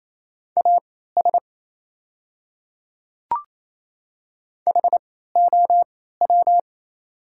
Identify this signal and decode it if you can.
Morse code, heard as a keyed tone, about 27 wpm, spaced as for 16 wpm, 690 Hz. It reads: AS HOW
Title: 3x Character Spacing - Top 100 Words - 28wpm